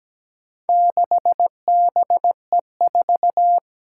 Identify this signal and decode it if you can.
Morse code, heard as a keyed tone, about 17 wpm, 700 Hz. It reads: 6BE4
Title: Four Challenging Letter-Number Combinations 17wpm